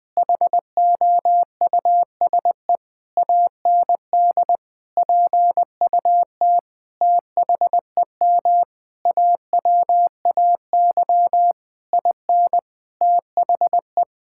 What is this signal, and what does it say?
Morse code, heard as a keyed tone, about 21 wpm, 700 Hz. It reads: HOUSE AND PUT THEM AWAY IN THE
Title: Sentences from Top 1000 Words - Review Part 5 - 20wpm